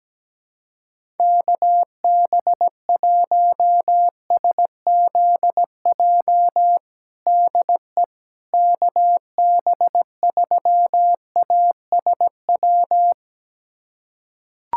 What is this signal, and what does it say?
Morse code, heard as a keyed tone, about 17 wpm, 700 Hz. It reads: KB1SZJ DE KB3ASW
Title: USA Callsign DE Callsign - 6 Characters - 17wpm